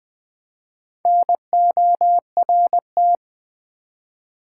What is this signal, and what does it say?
Morse code, heard as a keyed tone, about 20 wpm, 700 Hz. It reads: NORT
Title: Build A Word - 1 Letter At A Time - 5-Letter Words - 20wpm